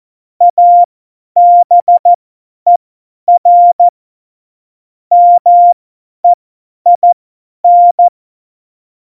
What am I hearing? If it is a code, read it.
Morse code, heard as a keyed tone, about 14 wpm, spaced as for 10 wpm, 700 Hz. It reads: ABER MEIN
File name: Morsekurs / Witze / Witz 10.Dezember 2019, 50 F70